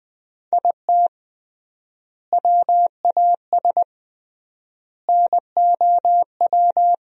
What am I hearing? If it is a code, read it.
Morse code, heard as a keyed tone, about 21 wpm, 700 Hz. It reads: IT WAS NOW
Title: 3x Word-Spacing - Sentences from Top 200 Words - 20wpm